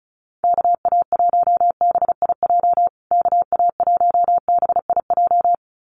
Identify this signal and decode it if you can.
Morse code, heard as a keyed tone, about 34 wpm, 700 Hz. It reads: KA1BIJ KA1BIJ
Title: USA Call Signs Repeated 2x 35wpm